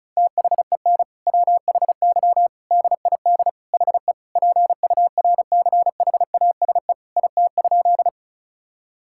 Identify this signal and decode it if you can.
Morse code, heard as a keyed tone, about 36 wpm, 690 Hz. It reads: THEN WHY DID HE PURCHASE IT?
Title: Sentences from Top 1000 Words - Review Part 5 - 35wpm